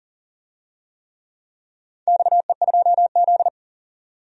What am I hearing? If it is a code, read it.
Morse code, heard as a keyed tone, about 39 wpm, 700 Hz. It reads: XE2Z
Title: DXCC-100 Call Signs - 4 Characters - 40wpm